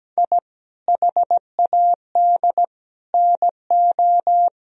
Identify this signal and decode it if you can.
Morse code, heard as a keyed tone, about 17 wpm, 700 Hz. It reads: I HAD NO
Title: Sentences from Top 1000 Words - Review Part 2 - 17wpm